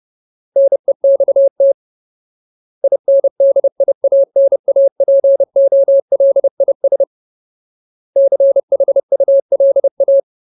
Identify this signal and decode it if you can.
Morse code, heard as a keyed tone, about 30 wpm, 550 Hz. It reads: NEXT INDIANAPOLIS CHULA
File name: Ultimate Word List / 30WPM Ultimate